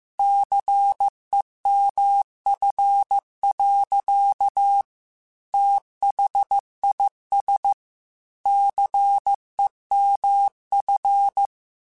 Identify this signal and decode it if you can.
Morse code, heard as a keyed tone, about 15 wpm, 800 Hz. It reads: CEMF. THIS CEMF
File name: Technical Subjects / Technical Subjects-mp3s / XL @13